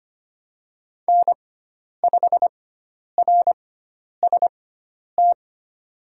Morse code, heard as a keyed tone, about 25 wpm, spaced as for 10 wpm, 700 Hz. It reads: N5RST
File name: Lesson 16g - Letter c - Call Signs 25wpm